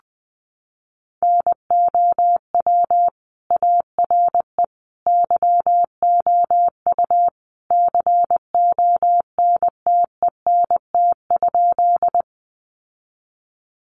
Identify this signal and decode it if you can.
Morse code, heard as a keyed tone, about 20 wpm, 700 Hz. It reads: NOW ARE YOU CONTENT?
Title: Sentences from Top 900 Words 20wpm